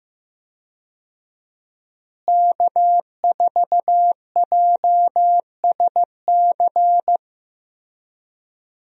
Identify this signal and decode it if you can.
Morse code, heard as a keyed tone, about 15 wpm, 700 Hz. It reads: K4JSC
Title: USA Call Signs - 5 Characters - Repeated 3x - 15wpm